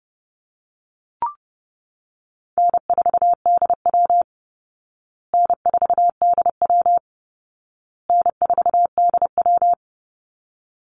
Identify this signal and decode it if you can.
Morse code, heard as a keyed tone, about 30 wpm, 700 Hz. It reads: N4DW N4DW N4DW
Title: Easy Call Signs Repeated 3x - 30wpm